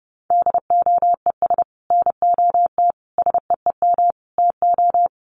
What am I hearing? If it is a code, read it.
Morse code, heard as a keyed tone, about 30 wpm, 700 Hz. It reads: DOES NOT SEEM TO
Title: Sets of 5 Words 30wpm